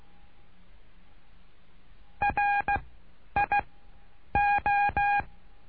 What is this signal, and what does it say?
Morse code, heard as a keyed tone, about 16 wpm, spaced as for 9 wpm, 820 Hz. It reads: RIO
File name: cursos / CURSO DE TELEGRAFIA / Telegrafia - Trilha 09